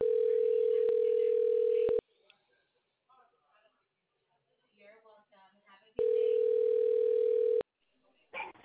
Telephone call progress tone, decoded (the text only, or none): ringback tone